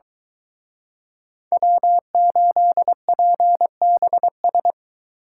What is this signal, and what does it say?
Morse code, heard as a keyed tone, about 24 wpm, 700 Hz. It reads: W8PBS